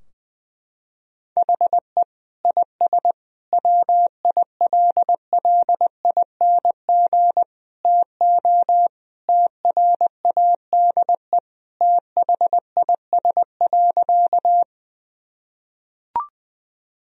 Morse code, heard as a keyed tone, about 20 wpm, 700 Hz. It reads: HE IS WILLING TO TRADE THIS.
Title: Sentences from Top 1000 Words - Review Part 4 - 20wpm